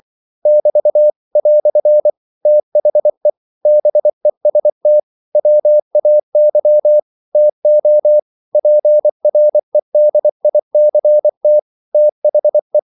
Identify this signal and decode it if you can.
Morse code, heard as a keyed tone, about 24 wpm, 600 Hz. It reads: <BT> "THE BEST WAY TO PREDICT THE
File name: CW / Practice / EnglishQuotations0424